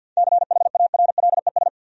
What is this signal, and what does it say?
Morse code, heard as a keyed tone, about 50 wpm, 700 Hz. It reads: CHARLES
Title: USA Common Names - 50wpm